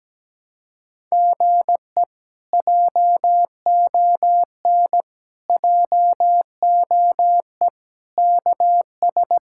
Morse code, heard as a keyed tone, about 17 wpm, 700 Hz. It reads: GE JON JOE KS
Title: K1USN SST QSO Practice - 17wpm